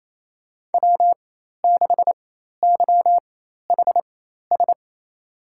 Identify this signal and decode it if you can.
Morse code, heard as a keyed tone, about 28 wpm, spaced as for 13 wpm, 700 Hz. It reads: W6YHS
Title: USA Call Signs - 5 Characters - 4x Character Spacing - 28wpm